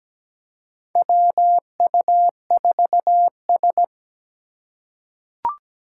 Morse code, heard as a keyed tone, about 17 wpm, 700 Hz. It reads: WU4S
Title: USA Call Signs - 4 Characters - 2x Character Spacing - 17wpm